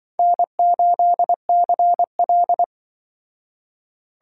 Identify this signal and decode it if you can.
Morse code, heard as a keyed tone, about 25 wpm, 700 Hz. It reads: N8CL
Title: Speed Racing - USA Call Signs 20wpm